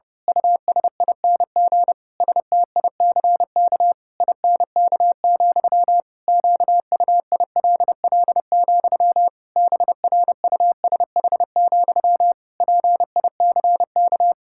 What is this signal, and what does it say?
Morse code, heard as a keyed tone, about 30 wpm, 700 Hz. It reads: USING STICK INK, QUILL, BRUSH, PICK
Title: Pangram Challenge v2 - 30wpm